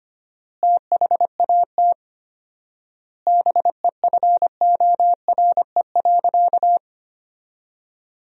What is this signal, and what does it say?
Morse code, heard as a keyed tone, about 25 wpm, 700 Hz. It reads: THAT BEFORE.
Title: 4x Word-Spacing - Sentences from Top 400 Words - 25wpm